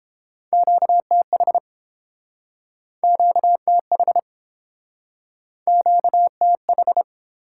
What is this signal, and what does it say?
Morse code, heard as a keyed tone, about 31 wpm, 700 Hz. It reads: QTH QTH QTH